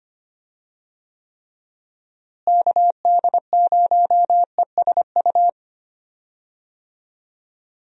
Morse code, heard as a keyed tone, about 25 wpm, 700 Hz. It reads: KD0ESU
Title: USA Call Signs - 6 Characters - Repeated 3x - 25wpm